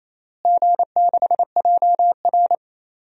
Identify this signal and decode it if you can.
Morse code, heard as a keyed tone, about 28 wpm, 700 Hz. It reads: G6JR
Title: DXCC-100 Call Signs - 4 Characters - Repeated 3x - 28wpm